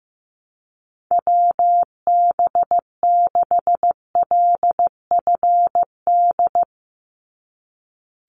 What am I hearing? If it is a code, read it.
Morse code, heard as a keyed tone, about 15 wpm, 700 Hz. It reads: WB6LFD
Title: USA Call Signs 15wpm